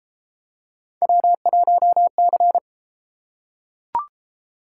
Morse code, heard as a keyed tone, about 31 wpm, 700 Hz. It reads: W1C